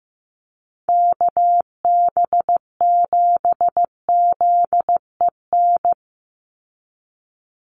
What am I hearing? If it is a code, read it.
Morse code, heard as a keyed tone, about 15 wpm, 700 Hz. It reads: KB7ZEN
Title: USA Call Signs 15wpm